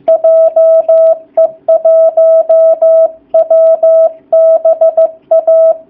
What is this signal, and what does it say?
Morse code, heard as a keyed tone, about 15 wpm, 660 Hz. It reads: JE1WBA
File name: jh8cbh / 2019cw / sp15- w